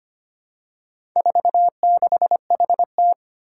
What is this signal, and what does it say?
Morse code, heard as a keyed tone, about 25 wpm, 700 Hz. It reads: 46HT
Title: Four Challenging Letter-Number Combinations 25wpm